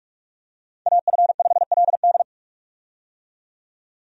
Morse code, heard as a keyed tone, about 46 wpm, 700 Hz. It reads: AF5LD
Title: USA Call Signs - 5 Characters - 45wpm